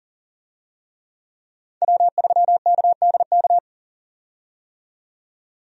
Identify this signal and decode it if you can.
Morse code, heard as a keyed tone, about 37 wpm, 700 Hz. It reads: W3KDK
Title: USA Call Signs - 5 Characters - 40wpm